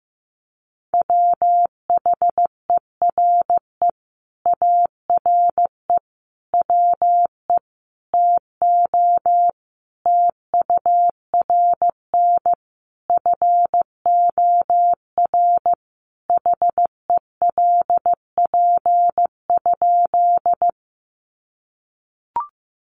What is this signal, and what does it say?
Morse code, heard as a keyed tone, about 15 wpm, 700 Hz. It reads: WHERE ARE WE TO TURN FOR HELP?
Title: Sentences from Top 200 Words 15wpm